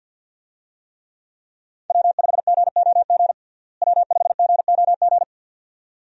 Morse code, heard as a keyed tone, about 51 wpm, 700 Hz. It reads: W5GOG W5GOG
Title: USA Call Signs - 5 Characters - Repeated 2x - 50wpm